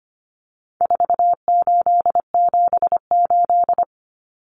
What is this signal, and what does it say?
Morse code, heard as a keyed tone, about 25 wpm, 700 Hz. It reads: 4878
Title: Common Cut Numbers - 4-Digit 25wpm